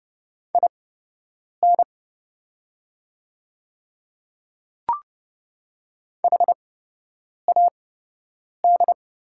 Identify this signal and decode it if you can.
Morse code, heard as a keyed tone, about 30 wpm, spaced as for 8 wpm, 700 Hz. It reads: IN HAD
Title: 8x Character Spacing - Top 100 Words - 30wpm